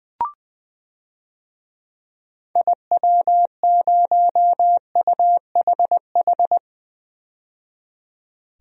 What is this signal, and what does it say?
Morse code, heard as a keyed tone, about 21 wpm, 700 Hz. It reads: IW0UHH
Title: DXCC-100 Call Signs - 6 Characters - 20wpm